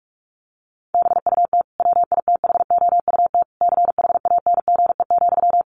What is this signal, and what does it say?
Morse code, heard as a keyed tone, about 44 wpm, 700 Hz. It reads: BUT WITHOUT CHANGE,
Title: Sentences from Top 1400 Words 45wpm